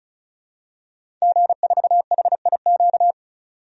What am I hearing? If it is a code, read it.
Morse code, heard as a keyed tone, about 34 wpm, 700 Hz. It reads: G4HIQ
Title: DXCC-100 Call Signs - 5 Characters - 35wpm